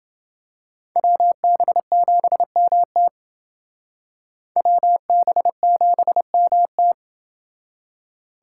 Morse code, heard as a keyed tone, about 28 wpm, 700 Hz. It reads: WB7MT WB7MT